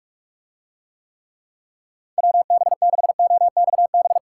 Morse code, heard as a keyed tone, about 46 wpm, 700 Hz. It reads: WB6OXB